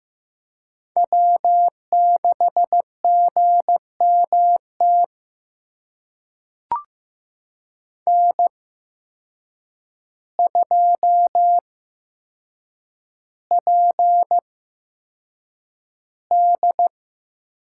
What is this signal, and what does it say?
Morse code, heard as a keyed tone, about 15 wpm, spaced as for 7 wpm, 700 Hz. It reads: W6GMT N2PD
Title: USA Call Signs - 5 Characters - 8x Character Spacing - 15wpm